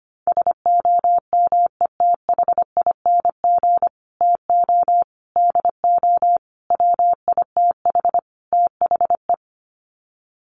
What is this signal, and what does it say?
Morse code, heard as a keyed tone, about 25 wpm, 690 Hz. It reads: SOMETHING TO DO WITH THE
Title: Sets of 5 Words Encore 25wpm